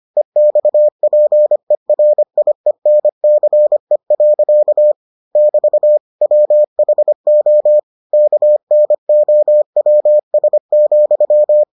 Morse code, heard as a keyed tone, about 25 wpm, 600 Hz. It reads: EXPERIENCE. <BT> WHO KNOWS,